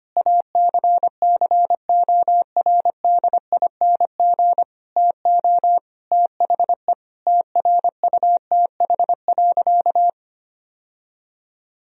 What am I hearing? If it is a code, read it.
Morse code, heard as a keyed tone, about 25 wpm, 700 Hz. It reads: ACCORDING TO THE TRUTH.
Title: Sentences from Top 1000 Words - Review Part 4 - 25wpm